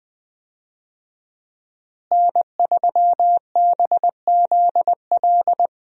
Morse code, heard as a keyed tone, about 21 wpm, 700 Hz. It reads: N3BZL
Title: USA Call Signs - 5 Characters - Repeated 3x - 20wpm